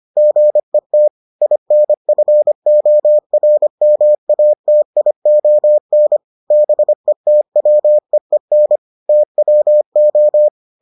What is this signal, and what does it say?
Morse code, heard as a keyed tone, about 25 wpm, 600 Hz. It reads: GET INFORMATION BETWEEN TWO